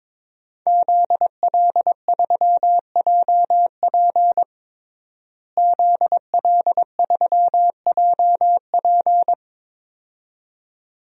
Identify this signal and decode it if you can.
Morse code, heard as a keyed tone, about 22 wpm, 700 Hz. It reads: ZL3JP ZL3JP
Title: DXCC-100 Call Signs - 5 Characters - Repeated 2x - 22wpm